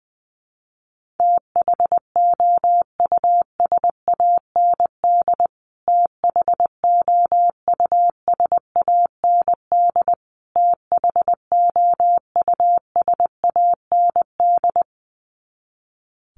Morse code, heard as a keyed tone, about 20 wpm, 700 Hz. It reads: THOUSAND THOUSAND THOUSAND
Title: Top 500 Words Repeated 20wpm